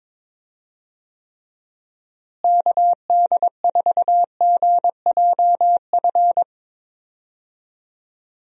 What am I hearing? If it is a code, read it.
Morse code, heard as a keyed tone, about 22 wpm, 700 Hz. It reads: KD4GJF
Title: USA Call Signs - 6 Characters - Repeated 3x - 22wpm